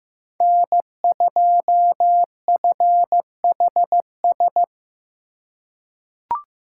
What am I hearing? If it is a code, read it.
Morse code, heard as a keyed tone, about 15 wpm, 700 Hz. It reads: N2FHS
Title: USA Call Signs - 2x Character Spacing - 15wpm